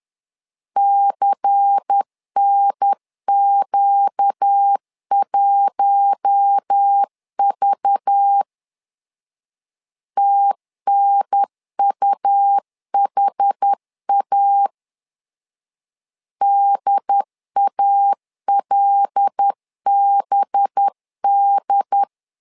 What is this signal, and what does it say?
Morse code, heard as a keyed tone, about 10 wpm, 800 Hz. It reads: CNQ1V TNUHA DALBD
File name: LEZIONI / 5 LBVC / 5 LBVC 10wpm 2349